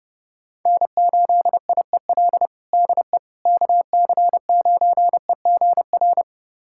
Morse code, heard as a keyed tone, about 30 wpm, 700 Hz. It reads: N8IEL DE KC9EGR
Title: USA Callsign DE Callsign 30wpm